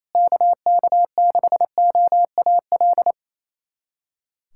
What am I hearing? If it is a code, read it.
Morse code, heard as a keyed tone, about 28 wpm, 700 Hz. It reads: KK6OAL
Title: USA Callsign DE Callsign - 6 Characters - 28wpm